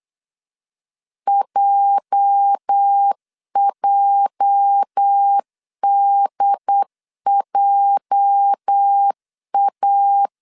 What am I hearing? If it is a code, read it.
Morse code, heard as a keyed tone, about 8 wpm, 800 Hz. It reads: JJDJA
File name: LEZIONI / 8 YJ45 / 8 YJ 8wpm 3313